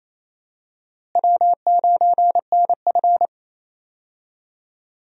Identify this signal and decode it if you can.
Morse code, heard as a keyed tone, about 28 wpm, 700 Hz. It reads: W9NF
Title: USA Call Signs - 4 Characters - Repeated 2x - 28wpm